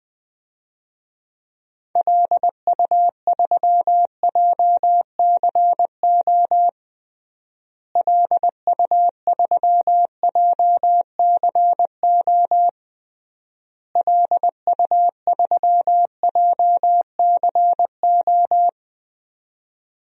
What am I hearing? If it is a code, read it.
Morse code, heard as a keyed tone, about 20 wpm, 700 Hz. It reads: LU3JCO LU3JCO LU3JCO
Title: DXCC-100 Call Signs - 6 Characters - Repeated 3x - 20wpm